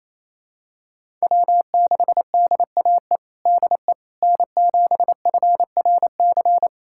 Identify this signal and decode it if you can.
Morse code, heard as a keyed tone, about 28 wpm, 700 Hz. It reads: W6DAE DE N7FRC
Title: USA Callsign DE Callsign - 5 Characters - 28wpm